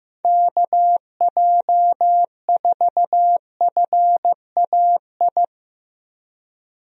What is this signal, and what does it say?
Morse code, heard as a keyed tone, about 15 wpm, 700 Hz. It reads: KJ4FAI
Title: USA Call Signs - 6 Characters - Repeated 3x - 15wpm